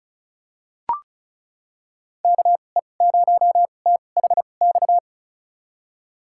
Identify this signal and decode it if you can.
Morse code, heard as a keyed tone, about 36 wpm, spaced as for 25 wpm, 700 Hz. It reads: KE0THX
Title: USA Call Signs - 2x Character Spacing - 35wpm